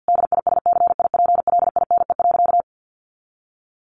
Morse code, heard as a keyed tone, about 50 wpm, 700 Hz. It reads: DISCIPLINE.
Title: Sentences from Top 900 Words 50wpm